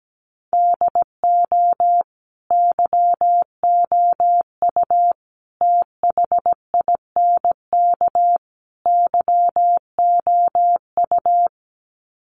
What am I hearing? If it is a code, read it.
Morse code, heard as a keyed tone, about 17 wpm, 700 Hz. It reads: DO YOU THINK YOU
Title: Sets of 4 Words 17wpm